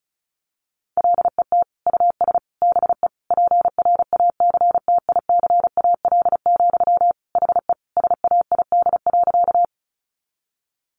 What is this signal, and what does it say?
Morse code, heard as a keyed tone, about 34 wpm, 700 Hz. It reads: LET US BE PRACTICAL, HE SAID.
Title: Sentences from Top 1400 Words 35wpm